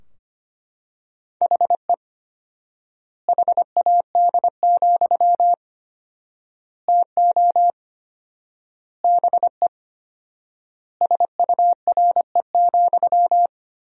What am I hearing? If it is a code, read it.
Morse code, heard as a keyed tone, about 25 wpm, 700 Hz. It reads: HE HAD, TO BE SURE,
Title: 4x Word-Spacing - Sentences from Top 500 Words - 25wpm